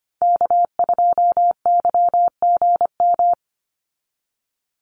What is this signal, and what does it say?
Morse code, heard as a keyed tone, about 25 wpm, 700 Hz. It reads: K2YGM
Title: Call Signs Repeated 3x - 25wpm